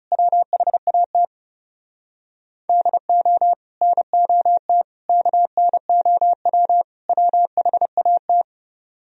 Morse code, heard as a keyed tone, about 40 wpm, 700 Hz. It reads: WHAT DO NOT KNOW WHAT